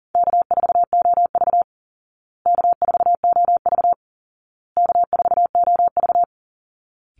Morse code, heard as a keyed tone, about 40 wpm, 700 Hz. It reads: K4OV K4OV K4OV
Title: Easy Call Signs Repeated 3x - 40wpm